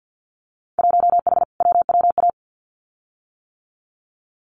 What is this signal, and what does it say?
Morse code, heard as a keyed tone, about 50 wpm, 700 Hz. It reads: 1H WWA
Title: Winter Field Day Exchanges 50wpm